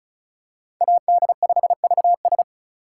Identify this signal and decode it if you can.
Morse code, heard as a keyed tone, about 36 wpm, 700 Hz. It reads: AD5VS
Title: USA Call Signs - 5 Characters - 3x Character Spacing - 35wpm